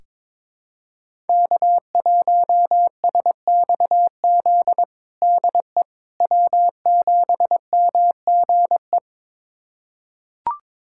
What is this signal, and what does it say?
Morse code, heard as a keyed tone, about 22 wpm, 700 Hz. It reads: K1SXZ DE W7MGE
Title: USA Callsign DE Callsign - 5 Characters - 22wpm